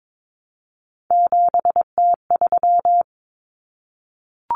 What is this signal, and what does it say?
Morse code, heard as a keyed tone, about 22 wpm, 700 Hz. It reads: 7T3
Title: Common Cut Numbers - 3-Digit 22wpm